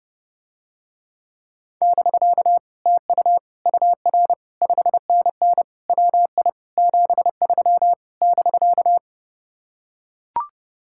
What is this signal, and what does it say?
Morse code, heard as a keyed tone, about 30 wpm, 700 Hz. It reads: <BK> TU UR 5NN WI 73 <BK>